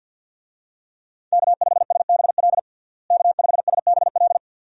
Morse code, heard as a keyed tone, about 50 wpm, 700 Hz. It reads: K5SBL K5SBL
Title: USA Call Signs - 5 Characters - Repeated 2x - 50wpm